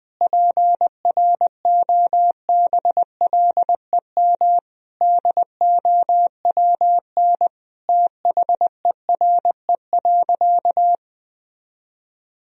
Morse code, heard as a keyed tone, about 20 wpm, 700 Hz. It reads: PROBLEM DOWN THERE.
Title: Sentences from Top 1000 Words - Review Part 4 - 20wpm